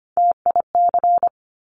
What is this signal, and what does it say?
Morse code, heard as a keyed tone, about 25 wpm, 700 Hz. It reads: TIC
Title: Top Three Letter Combinations in Words 25wpm